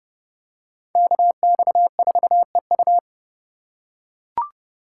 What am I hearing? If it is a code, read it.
Morse code, heard as a keyed tone, about 29 wpm, 700 Hz. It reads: KX4EU